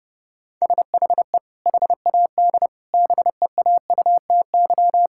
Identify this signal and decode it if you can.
Morse code, heard as a keyed tone, about 30 wpm, 700 Hz. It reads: SHE HAD BEAUTY
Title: Sentences from Top 1000 Words - Review Part 5 - 30wpm